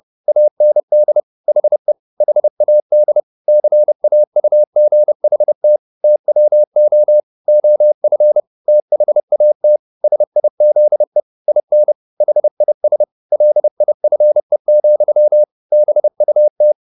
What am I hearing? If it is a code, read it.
Morse code, heard as a keyed tone, about 30 wpm, 600 Hz. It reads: AND HE HAD CAUGHT TWO OF THAT SIZE IN HIS LIFE, BUT
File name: the old man and the sea / the old man and the sea 1 30wpm